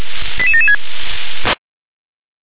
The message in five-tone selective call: 88776

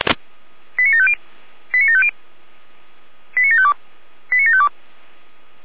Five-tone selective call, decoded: 89750 89750 89752 89752

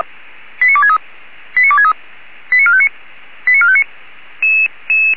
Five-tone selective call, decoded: 89263 89263 89459 89459

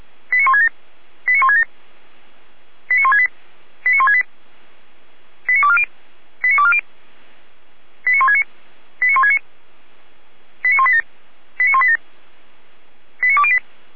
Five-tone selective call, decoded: 89167 89167 89168 89168 89240 89240 89169 89169 89187 89187 89208

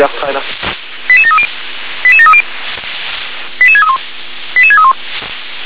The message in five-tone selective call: 88730 88730 88631 88631

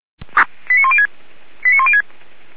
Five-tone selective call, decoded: 89106 89106

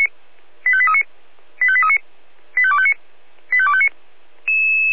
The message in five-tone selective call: 85839 85839 85259 85259

